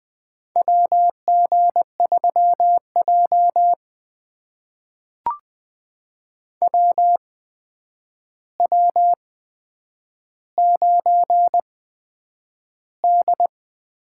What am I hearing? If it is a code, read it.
Morse code, heard as a keyed tone, about 20 wpm, spaced as for 9 wpm, 700 Hz. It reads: WG3J WW9D
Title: USA Call Signs - 4 Characters - 8x Character Spacing - 20wpm